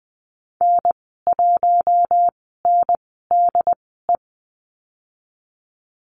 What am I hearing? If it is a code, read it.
Morse code, heard as a keyed tone, about 20 wpm, spaced as for 15 wpm, 700 Hz. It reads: N1NDE
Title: 069 Lesson 12g - Letter d - Call Signs 20wpm